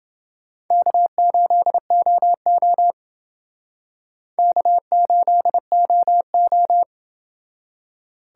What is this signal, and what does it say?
Morse code, heard as a keyed tone, about 29 wpm, 700 Hz. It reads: K8OO K8OO